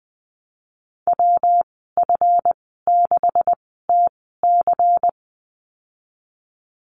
Morse code, heard as a keyed tone, about 20 wpm, spaced as for 15 wpm, 700 Hz. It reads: WF6TC
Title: 147 Lesson 24g - Letter y - Call Signs 20wpm